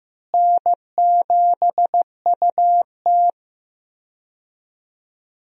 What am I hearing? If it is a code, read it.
Morse code, heard as a keyed tone, about 15 wpm, 700 Hz. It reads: N7UT
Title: USA Call Signs - 4 Characters - Repeated 2x - 15wpm